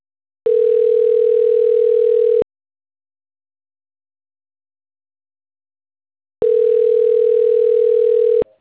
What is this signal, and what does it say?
Ringback tone (North America), 440 + 480 Hz.